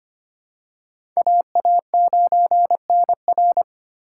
Morse code, heard as a keyed tone, about 25 wpm, 700 Hz. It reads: AA9NR
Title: USA Call Signs - 5 Characters - 6x Character Spacing - 25wpm